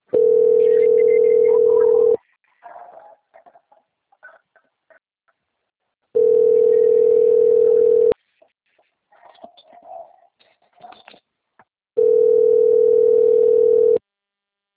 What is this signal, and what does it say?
Ringback tone (North America), 440 + 480 Hz.